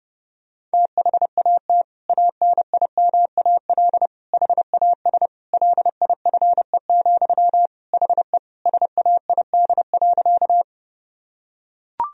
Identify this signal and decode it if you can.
Morse code, heard as a keyed tone, about 30 wpm, 700 Hz. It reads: THAT ANIMAL HAS LIFE, HE SAID.
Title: Sentences from Top 1000 Words - Review Part 1 - 30wpm